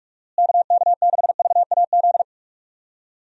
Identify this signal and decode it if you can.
Morse code, heard as a keyed tone, about 43 wpm, 700 Hz. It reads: KK6VAZ
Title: USA Call Signs - 6 Characters - Repeated 3x - 45wpm